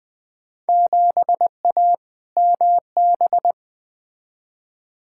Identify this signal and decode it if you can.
Morse code, heard as a keyed tone, about 20 wpm, 700 Hz. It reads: 7A MB
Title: Field Day Exchanges Encore - 20wpm